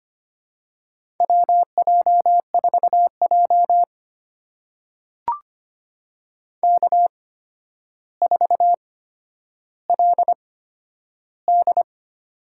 Morse code, heard as a keyed tone, about 25 wpm, spaced as for 11 wpm, 700 Hz. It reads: WJ4J K4LD
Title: USA Call Signs - 4 Characters - 8x Character Spacing - 25wpm